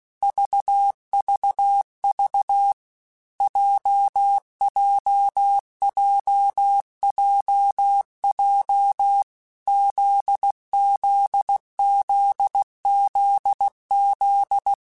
Morse code, heard as a keyed tone, about 16 wpm, 800 Hz. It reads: VVV JJJJJ ZZZZZ
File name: Basic Letter Groups By7Groups / JZ17 / JZ17@15